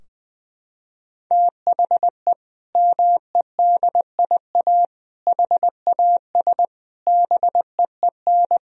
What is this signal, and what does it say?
Morse code, heard as a keyed tone, about 20 wpm, 700 Hz. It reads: THE MEDIA HAS BEEN
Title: Sentences from Top 1000 Words - Review Part 4 - 20wpm